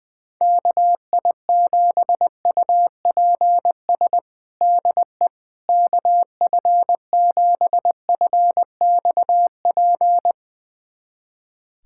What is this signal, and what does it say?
Morse code, heard as a keyed tone, about 20 wpm, 700 Hz. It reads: KI7UPS DE KF7FXP
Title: USA Callsign DE Callsign - 6 Characters - 20wpm